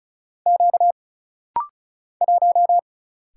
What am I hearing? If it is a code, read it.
Morse code, heard as a keyed tone, about 35 wpm, 700 Hz. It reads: Q 1
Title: Single Letter-Number - Rapid-Fire 35wpm